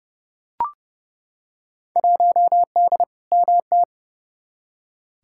Morse code, heard as a keyed tone, about 29 wpm, 700 Hz. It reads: 1D MT
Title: Field Day Exchanges - 30wpm